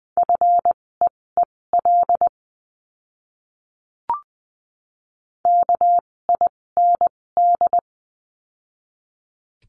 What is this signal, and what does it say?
Morse code, heard as a keyed tone, about 20 wpm, spaced as for 16 wpm, 700 Hz. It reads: FEEL KIND
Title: 256 Lesson 42a - Words 12-20wpm